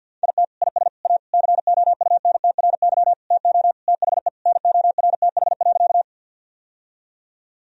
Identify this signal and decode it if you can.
Morse code, heard as a keyed tone, about 52 wpm, 700 Hz. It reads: IT IS A COUNTRY TO THE NORTH.
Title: Sentences from Top 1000 Words - Review Part 1 - 50wpm